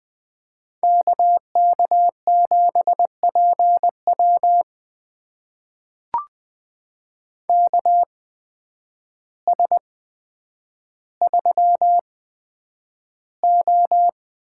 Morse code, heard as a keyed tone, about 20 wpm, spaced as for 9 wpm, 700 Hz. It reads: KK7PW KS3O
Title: USA Call Signs - 8x Character Spacing - 20wpm